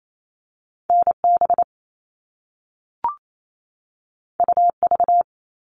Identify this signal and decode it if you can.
Morse code, heard as a keyed tone, about 28 wpm, 700 Hz. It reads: NB UV
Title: All Cut Numbers - 2-Digit 28wpm